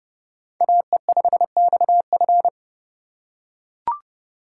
Morse code, heard as a keyed tone, about 28 wpm, 700 Hz. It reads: AE5XF